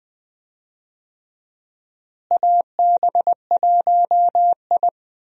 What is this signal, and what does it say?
Morse code, heard as a keyed tone, about 20 wpm, 700 Hz. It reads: AB1I
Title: USA Call Signs - 4 Characters - Repeated 3x - 20wpm